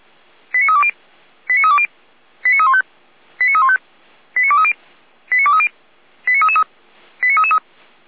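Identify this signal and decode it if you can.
Five-tone selective call (ZVEI), heard as quick runs of five tones, tones of about 70 ms. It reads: 89320 89320 89315 89315 89230 89230 89333 89333
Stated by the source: ZVEI five-tone call received by SDR, from the feuerwehr-orlamuende.org audio archive.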